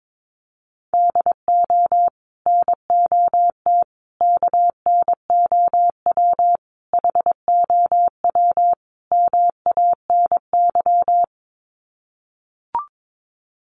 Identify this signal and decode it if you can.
Morse code, heard as a keyed tone, about 22 wpm, 700 Hz. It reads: DO NOT KNOW HOW MANY
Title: Sets of 5 Words Encore 22wpm